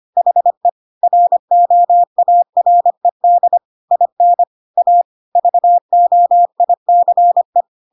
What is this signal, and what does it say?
Morse code, heard as a keyed tone, about 25 wpm, 700 Hz. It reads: HE ROARED IN A VOICE
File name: Pinocchio / Pinocchio ch3 25 25 ik1hge